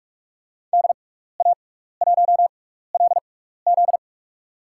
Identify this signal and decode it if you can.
Morse code, heard as a keyed tone, about 45 wpm, spaced as for 15 wpm, 700 Hz. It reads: DA1LZ
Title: USA Call Signs - 5 Characters - 6x Character Spacing - 45wpm